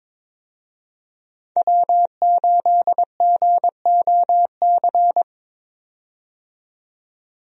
Morse code, heard as a keyed tone, about 22 wpm, 700 Hz. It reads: W8GOC
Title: USA Call Signs - 5 Characters - 22wpm